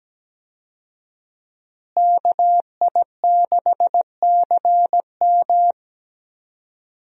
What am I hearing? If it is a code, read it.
Morse code, heard as a keyed tone, about 17 wpm, 700 Hz. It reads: KI6CM
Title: USA Call Signs - 5 Characters - Repeated 3x - 17wpm